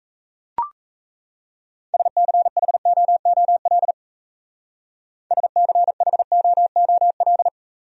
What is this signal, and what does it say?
Morse code, heard as a keyed tone, about 34 wpm, 700 Hz. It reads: SCHOOL SCHOOL
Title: Speed-Racing Top 500 Words 28wpm